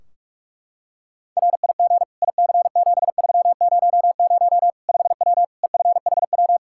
Morse code, heard as a keyed tone, about 46 wpm, 700 Hz. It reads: RIG IC7300 5W EFHW
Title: Practice QSO - Exchange 2 - Info - 45wpm